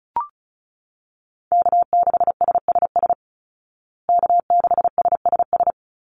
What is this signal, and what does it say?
Morse code, heard as a keyed tone, about 34 wpm, 700 Hz. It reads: K6SSS K6SSS
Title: Call Signs Repeated 3x - 35wpm